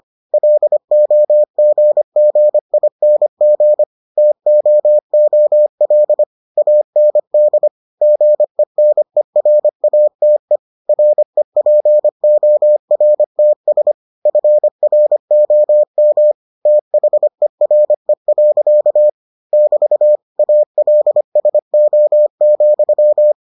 Morse code, heard as a keyed tone, about 25 wpm, 600 Hz. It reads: LOGGING TOOL AND GENERATE REPORTS FROM THERE. <BT> ALSO,